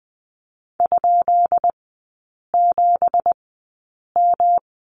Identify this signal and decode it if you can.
Morse code, heard as a keyed tone, about 20 wpm, spaced as for 8 wpm, 700 Hz. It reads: ?7M
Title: 172 Lesson 28e - Number 9 - Three Characters 20wpm